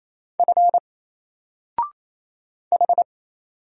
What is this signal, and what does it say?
Morse code, heard as a keyed tone, about 28 wpm, 700 Hz. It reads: F H
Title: Single Character (Letter:Number:Punctuation:Prosign) 28wpm